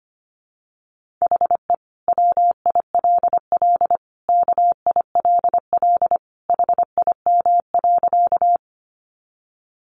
Morse code, heard as a keyed tone, about 25 wpm, 700 Hz. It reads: HE WILL KILL HIM.
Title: Sentences from Top 1400 Words 25wpm